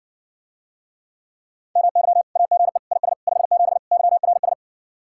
Morse code, heard as a keyed tone, about 60 wpm, 700 Hz. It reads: MY AGE IS 57 YRS